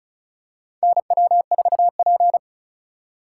Morse code, heard as a keyed tone, about 36 wpm, 700 Hz. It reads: NW4P
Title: USA Call Signs - 4 Characters - 4x Character Spacing - 35wpm